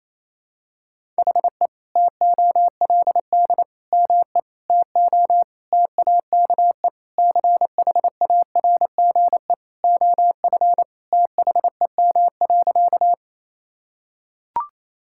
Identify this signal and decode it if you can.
Morse code, heard as a keyed tone, about 28 wpm, 700 Hz. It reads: HE TOLD ME TO TAKE CHARGE OF THEM.
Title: Sentences from Top 1000 Words - Review Part 6 - 28wpm